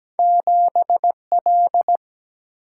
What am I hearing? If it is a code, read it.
Morse code, heard as a keyed tone, about 17 wpm, 700 Hz. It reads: 7L
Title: Single-Letter Single-Number 17wpm